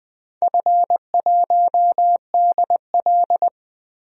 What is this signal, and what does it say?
Morse code, heard as a keyed tone, about 21 wpm, 700 Hz. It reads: F1DL
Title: DXCC-100 Call Signs - 4 Characters - 20wpm